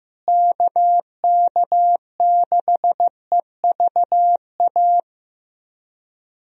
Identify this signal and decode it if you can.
Morse code, heard as a keyed tone, about 15 wpm, 700 Hz. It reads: KK6EVA
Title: USA Call Signs - 6 Characters - Repeated 2x - 15wpm